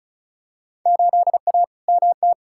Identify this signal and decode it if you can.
Morse code, heard as a keyed tone, about 36 wpm, 700 Hz. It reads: 8A MT
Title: Field Day Exchanges Encore - 35wpm